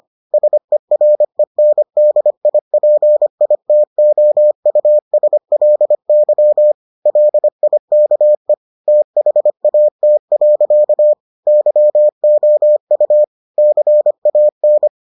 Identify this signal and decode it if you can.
Morse code, heard as a keyed tone, about 25 wpm, 600 Hz. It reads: SERENDIPITOUSLY LIKE THAT. YOU CAN